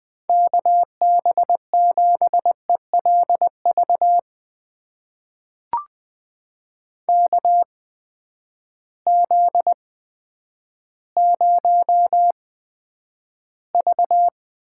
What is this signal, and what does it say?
Morse code, heard as a keyed tone, about 20 wpm, spaced as for 10 wpm, 700 Hz. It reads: KB7ELV KZ0V
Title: USA Call Signs - 8x Character Spacing - 20wpm